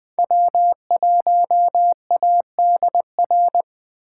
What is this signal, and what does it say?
Morse code, heard as a keyed tone, about 21 wpm, 700 Hz. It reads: W1ADR